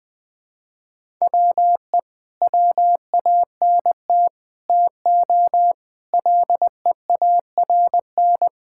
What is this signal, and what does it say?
Morse code, heard as a keyed tone, about 20 wpm, 700 Hz. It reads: WE WANT TO LEARN
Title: Sentences with 3-Word Phrases - 20wpm